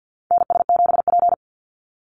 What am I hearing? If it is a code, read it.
Morse code, heard as a keyed tone, about 48 wpm, 700 Hz. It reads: NS7P
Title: NA SOTA Activator Callsigns 50wpm